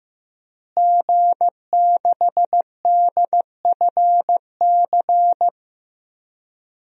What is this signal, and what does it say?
Morse code, heard as a keyed tone, about 15 wpm, 700 Hz. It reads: G6DFC
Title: DXCC-100 Call Signs - 5 Characters - Repeated 3x - 15wpm